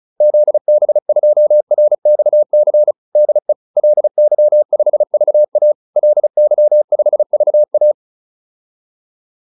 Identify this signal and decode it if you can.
Morse code, heard as a keyed tone, about 34 wpm, 600 Hz. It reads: ZB2RXC DE LY5VA LY5VA
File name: QSO406 35